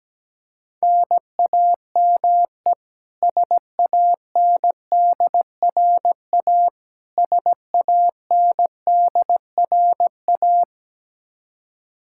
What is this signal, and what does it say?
Morse code, heard as a keyed tone, about 17 wpm, 700 Hz. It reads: NAME SANDRA SANDRA
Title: Practice QSO - Exchange 1 - Beginning - 17wpm